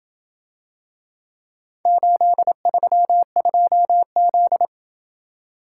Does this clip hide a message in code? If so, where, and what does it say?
Yes: Morse code from 1.9 to 4.7 s, heard as a keyed tone, about 28 wpm, 700 Hz. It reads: 832Z